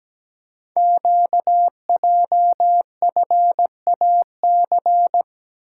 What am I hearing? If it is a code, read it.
Morse code, heard as a keyed tone, about 17 wpm, 700 Hz. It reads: QJFAC
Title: Code Groups - 5 Letters - 17wpm